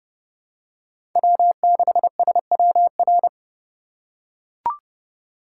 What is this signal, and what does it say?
Morse code, heard as a keyed tone, about 30 wpm, 700 Hz. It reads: W6SWR
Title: USA Call Signs - 5 Characters - 2x Character Spacing - 30wpm